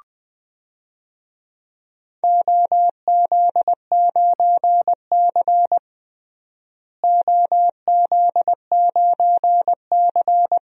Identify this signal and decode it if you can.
Morse code, heard as a keyed tone, about 20 wpm, 700 Hz. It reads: OZ9C OZ9C
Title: DXCC-100 Call Signs - 4 Characters - Repeated 2x - 20wpm